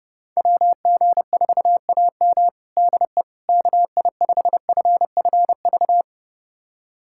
Morse code, heard as a keyed tone, about 30 wpm, 700 Hz. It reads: WG4AM DE KI5FFV
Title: USA Callsign DE Callsign 30wpm